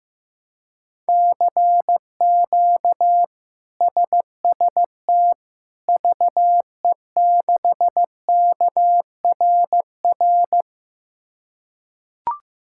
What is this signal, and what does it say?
Morse code, heard as a keyed tone, about 15 wpm, 700 Hz. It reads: CQ SST VE6KRR
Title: K1USN SST QSO Practice - 15wpm